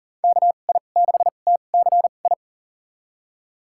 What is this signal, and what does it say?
Morse code, heard as a keyed tone, about 40 wpm, spaced as for 29 wpm, 700 Hz. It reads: KI6TCI
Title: USA Call Signs - 2x Character Spacing - 40wpm